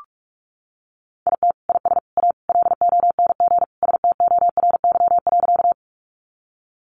Morse code, heard as a keyed tone, about 44 wpm, 700 Hz. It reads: IT IS A LONG STORY.
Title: Sentences from Top 300 Words 45wpm